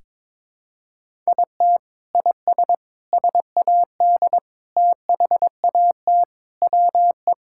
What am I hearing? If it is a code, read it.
Morse code, heard as a keyed tone, about 22 wpm, 700 Hz. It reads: IT IS SAD THAT WE
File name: practice / 22wpm / Sentences with 4-Word Phrases - 22wpm